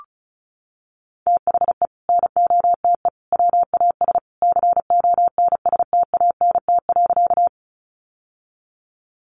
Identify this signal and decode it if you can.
Morse code, heard as a keyed tone, about 34 wpm, 700 Hz. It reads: THE NOTE WAS CONSTANT.
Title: Sentences from Top 1500 Words 35wpm